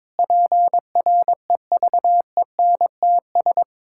Morse code, heard as a keyed tone, about 22 wpm, 700 Hz. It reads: PREVENTS
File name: Difficult Sentences with Hints - 22wpm